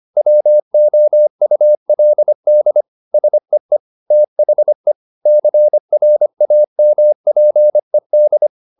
Morse code, heard as a keyed tone, about 25 wpm, 600 Hz. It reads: WOULD SEE THE CRAMPED
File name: the old man and the sea part 1 25wpm